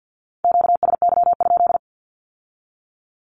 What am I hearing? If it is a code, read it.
Morse code, heard as a keyed tone, about 50 wpm, 700 Hz. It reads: QSY?
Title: Q-Codes with Spoken Meaning - 50wpm